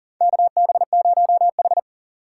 Morse code, heard as a keyed tone, about 40 wpm, 690 Hz. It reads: KB0H
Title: USA Call Signs - 3x Character Spacing - 40wpm